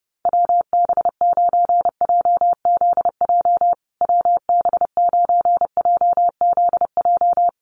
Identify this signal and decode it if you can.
Morse code, heard as a keyed tone, about 30 wpm, 700 Hz. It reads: WB9JZJ WB9JZJ
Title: USA Call Signs Repeated 3x 30wpm